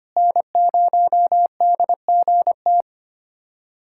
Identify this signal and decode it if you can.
Morse code, heard as a keyed tone, about 25 wpm, 700 Hz. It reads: N0DGT
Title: USA Callsign DE Callsign - 5 Characters - 25wpm